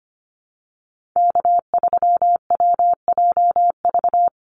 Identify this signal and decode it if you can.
Morse code, heard as a keyed tone, about 25 wpm, 700 Hz. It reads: K3WJV
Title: Call Signs Repeated 3x - 25wpm